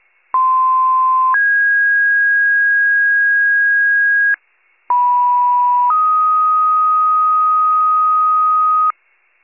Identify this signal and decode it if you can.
Two-tone page, heard as a one-second tone, then a three-second tone, about 1040 then 1650 Hz, about 980 then 1250 Hz, with noise. It is heard from 0.3 to 8.9 s.